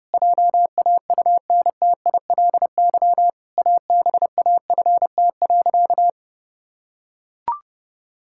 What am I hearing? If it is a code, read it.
Morse code, heard as a keyed tone, about 30 wpm, 700 Hz. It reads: JAUNTILY ABAFT.
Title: Pangram Challenge v2 - 30wpm